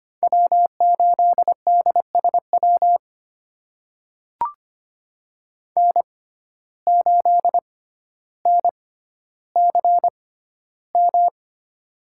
Morse code, heard as a keyed tone, about 25 wpm, spaced as for 13 wpm, 700 Hz. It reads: W8DSW N8NCM
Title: USA Call Signs - 5 Characters - 6x Character Spacing - 25wpm